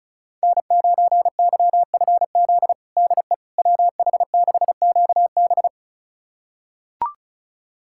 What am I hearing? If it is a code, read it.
Morse code, heard as a keyed tone, about 36 wpm, 700 Hz. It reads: N9YFZ DE WH6QB
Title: USA Callsign DE Callsign - 5 Characters - 35wpm